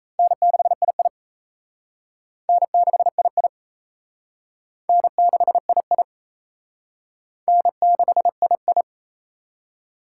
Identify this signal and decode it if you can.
Morse code, heard as a keyed tone, about 34 wpm, 700 Hz. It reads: N6II N6II N6II N6II
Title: Speed Racing - USA 1x2 Call Signs 28wpm